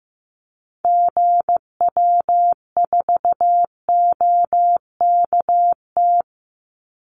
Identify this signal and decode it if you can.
Morse code, heard as a keyed tone, about 15 wpm, 700 Hz. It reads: GW4OKT
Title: Call Signs Repeated 2x - 15wpm